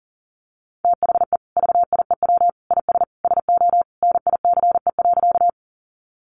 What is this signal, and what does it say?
Morse code, heard as a keyed tone, about 37 wpm, 700 Hz. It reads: THE VIEW IS SO NICE.
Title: Additional Easy Phrases 40wpm